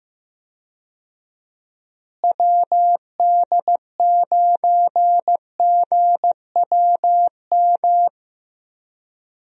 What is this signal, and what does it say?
Morse code, heard as a keyed tone, about 15 wpm, 700 Hz. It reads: WD9GWM
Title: USA Call Signs - 6 Characters - Repeated 2x - 15wpm